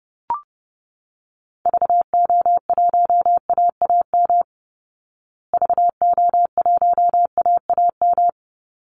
Morse code, heard as a keyed tone, about 30 wpm, 700 Hz. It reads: VO1AAM VO1AAM
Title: Call Signs Repeated 2x - 30wpm